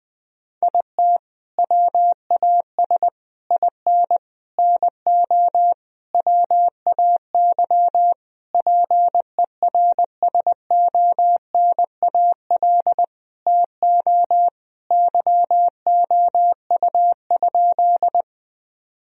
Morse code, heard as a keyed tone, about 21 wpm, 700 Hz. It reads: IT WAS IN NO WAY PERSONAL TO YOU?
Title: Sentences from Top 1000 Words - Review Part 4 - 20wpm